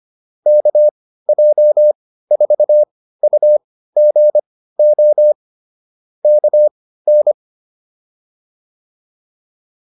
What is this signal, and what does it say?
Morse code, heard as a keyed tone, about 25 wpm, spaced as for 15 wpm, 600 Hz. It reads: KJ4UGO KN